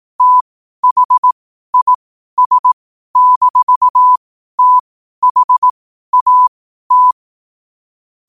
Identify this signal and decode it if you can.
Morse code, heard as a keyed tone, about 18 wpm, spaced as for 12 wpm, 1000 Hz. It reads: THIS-THAT
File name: qotdfiles / 12wpm